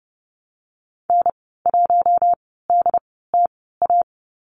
Morse code, heard as a keyed tone, about 30 wpm, spaced as for 17 wpm, 700 Hz. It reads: N1DTA
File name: course / 30:10wpm / 088 Lesson 15g - Number 5 - Call Signs 30wpm